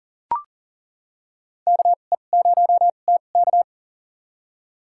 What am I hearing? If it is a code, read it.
Morse code, heard as a keyed tone, about 40 wpm, spaced as for 29 wpm, 700 Hz. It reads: KE0TK
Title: USA Call Signs - 5 Characters - 2x Character Spacing - 40wpm